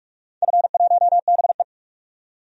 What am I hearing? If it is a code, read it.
Morse code, heard as a keyed tone, about 42 wpm, 700 Hz. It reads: F1BE
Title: DXCC-100 Call Signs - 4 Characters - 45wpm